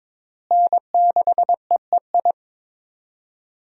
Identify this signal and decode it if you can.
Morse code, heard as a keyed tone, about 22 wpm, 700 Hz. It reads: N6EEI
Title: USA Call Signs - 5 Characters - Repeated 2x - 22wpm